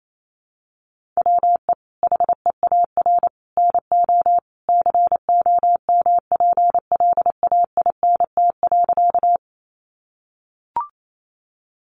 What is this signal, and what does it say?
Morse code, heard as a keyed tone, about 28 wpm, 700 Hz. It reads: WE HEAR NO COMPLAINT.
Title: Sentences from Top 1400 Words 28wpm